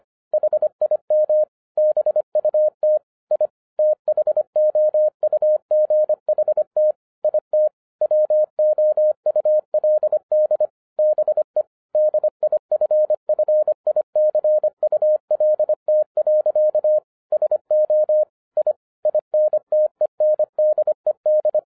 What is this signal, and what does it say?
Morse code, heard as a keyed tone, about 24 wpm, 610 Hz. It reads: HIM BUT I THOUGHT IT WOULD BE DIFFICULT. SO I INTENDED